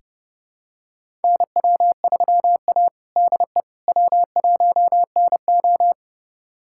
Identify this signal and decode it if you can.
Morse code, heard as a keyed tone, about 30 wpm, 700 Hz. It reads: NW3A DE W1NO
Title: USA Callsign DE Callsign - 4 Characters - 30wpm